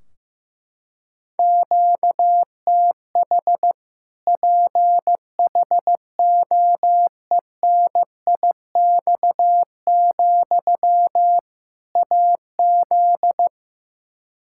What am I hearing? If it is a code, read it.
Morse code, heard as a keyed tone, about 15 wpm, 700 Hz. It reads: QTH PHOENIX, AZ
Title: Practice QSO - Exchange 1 - Beginning - 15wpm